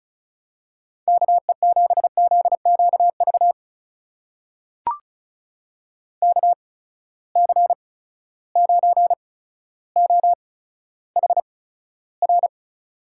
Morse code, heard as a keyed tone, about 36 wpm, spaced as for 15 wpm, 700 Hz. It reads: KE7ZQV KC9OHR
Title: USA Call Signs - 6 Characters - 8x Character Spacing - 35wpm